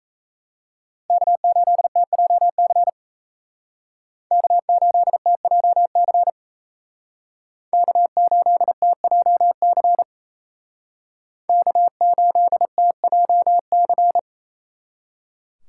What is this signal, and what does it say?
Morse code, heard as a keyed tone, about 35 wpm, 700 Hz. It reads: K8TJC K8TJC K8TJC K8TJC
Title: Speed Racing - USA Call Signs 28wpm